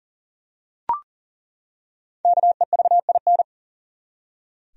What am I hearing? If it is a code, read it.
Morse code, heard as a keyed tone, about 40 wpm, 700 Hz. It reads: KEVIN
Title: USA Common Names - Max 5 Letters - 40wpm